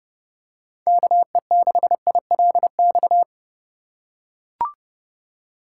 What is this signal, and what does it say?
Morse code, heard as a keyed tone, about 30 wpm, 700 Hz. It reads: KE6ILX
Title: USA Call Signs - 6 Characters - 3x Character Spacing - 30wpm